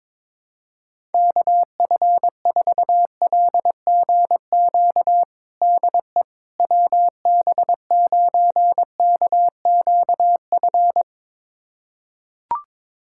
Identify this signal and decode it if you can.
Morse code, heard as a keyed tone, about 22 wpm, 700 Hz. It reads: KF4LGQ DE WB9KQF
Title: USA Callsign DE Callsign - 6 Characters - 22wpm